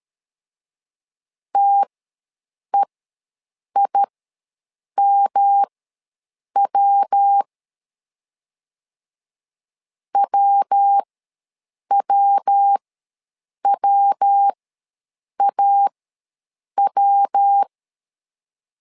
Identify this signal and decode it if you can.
Morse code, heard as a keyed tone, about 13 wpm, spaced as for 7 wpm, 800 Hz. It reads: TEIMW WWWAW
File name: LEZIONI / 3 MWRO / 3 MW ws 12wpm 1816